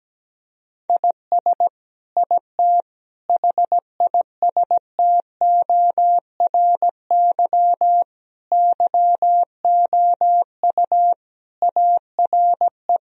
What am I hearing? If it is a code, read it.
Morse code, heard as a keyed tone, about 17 wpm, 700 Hz. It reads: IS IT HISTORY YOU ARE
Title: Sentences from Top 1000 Words - Review Part 3 - 17wpm